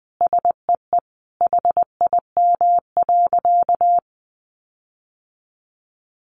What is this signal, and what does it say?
Morse code, heard as a keyed tone, about 19 wpm, 700 Hz. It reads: SEE HIM.
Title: Sentences from Top 1500 Words 20wpm